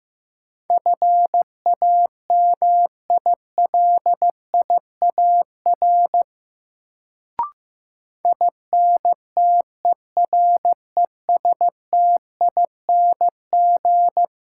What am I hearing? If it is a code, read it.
Morse code, heard as a keyed tone, about 15 wpm, 700 Hz. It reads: FAMILIAR INTERESTING
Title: Sentences from Top 1000 Words - New Word Drill - 15wpm